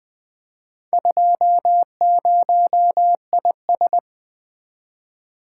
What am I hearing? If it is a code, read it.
Morse code, heard as a keyed tone, about 20 wpm, 700 Hz. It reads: 20IS
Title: Four Letter-Number Combinations - 20wpm